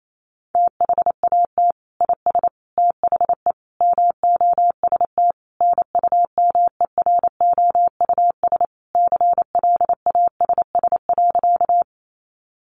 Morse code, heard as a keyed tone, about 28 wpm, 700 Hz. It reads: THAT IS THE MOST NUMEROUS CLASS.
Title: Sentences from Top 1400 Words 28wpm